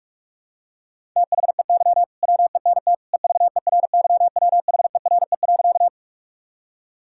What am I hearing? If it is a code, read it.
Morse code, heard as a keyed tone, about 44 wpm, 700 Hz. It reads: THEY WENT EVERYWHERE.
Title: Sentences from Top 1600 Words 45wpm